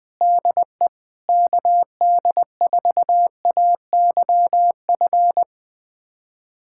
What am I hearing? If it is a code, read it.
Morse code, heard as a keyed tone, about 21 wpm, 700 Hz. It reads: DE KD4AYF
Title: USA Callsign DE Callsign - 6 Characters - 20wpm